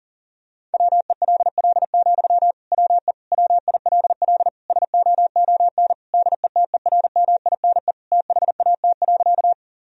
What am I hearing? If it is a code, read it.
Morse code, heard as a keyed tone, about 40 wpm, 700 Hz. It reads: WELL, WE WILL SOON DETERMINE THAT.
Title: Sentences from Top 1000 Words - Review Part 4 - 40wpm